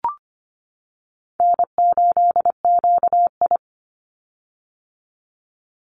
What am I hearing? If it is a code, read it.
Morse code, heard as a keyed tone, about 25 wpm, 700 Hz. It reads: N8QI
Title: USA Call Signs 25wpm